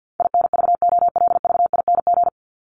Morse code, heard as a keyed tone, about 50 wpm, 700 Hz. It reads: INVOLVING
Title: ING Suffix - Top 100 Words with Base Words 50wpm